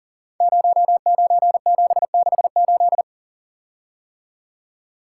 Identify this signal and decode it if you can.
Morse code, heard as a keyed tone, about 39 wpm, 700 Hz. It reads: 09768
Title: Code Groups - 5 Letter:Numbers - 40wpm